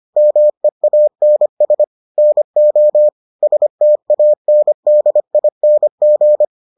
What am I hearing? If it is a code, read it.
Morse code, heard as a keyed tone, about 25 wpm, 600 Hz. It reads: MEANS NO STANDING